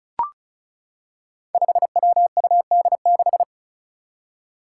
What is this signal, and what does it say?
Morse code, heard as a keyed tone, about 34 wpm, 700 Hz. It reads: 5WUD6
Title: Five Challenging Letter-Number Combinations 35wpm